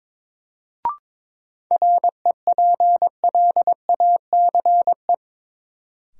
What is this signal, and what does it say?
Morse code, heard as a keyed tone, about 22 wpm, 700 Hz. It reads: REPLACE
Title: Sentences from Top 1400 Words - New Word Drill - 22wpm